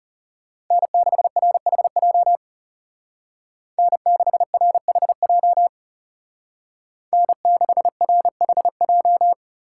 Morse code, heard as a keyed tone, about 36 wpm, 700 Hz. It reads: N6RHJ N6RHJ N6RHJ
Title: Speed Racing - USA Call Signs 30wpm